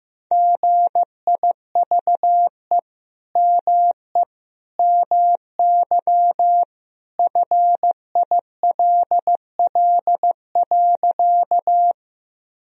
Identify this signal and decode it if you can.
Morse code, heard as a keyed tone, about 15 wpm, 700 Hz. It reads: GIVE ME MY FILL.
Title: Sentences from Top 1000 Words - Review Part 3 - 15wpm